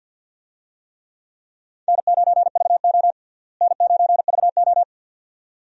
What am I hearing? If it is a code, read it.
Morse code, heard as a keyed tone, about 51 wpm, 700 Hz. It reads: N9VO N9VO
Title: USA Call Signs - 4 Characters - Repeated 2x - 50wpm